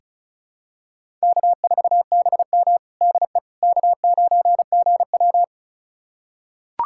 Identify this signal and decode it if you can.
Morse code, heard as a keyed tone, about 36 wpm, 700 Hz. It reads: K4BM DE K9GW
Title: USA Callsign DE Callsign - 4 Characters - 35wpm